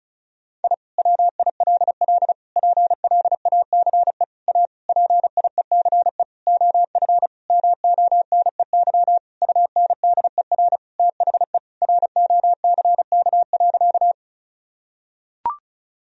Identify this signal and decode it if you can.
Morse code, heard as a keyed tone, about 34 wpm, 700 Hz. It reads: I WILL PLACE A PIECE OF MONEY UNDER THE ROCK.
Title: Sentences from Top 1000 Words - Review Part 2 - 35wpm